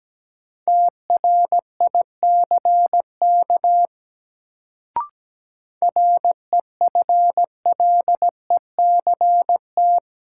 Morse code, heard as a keyed tone, about 17 wpm, 700 Hz. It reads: TRICK REFLECT
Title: Sentences from Top 1500 Words - New Word Drill Alt - 17wpm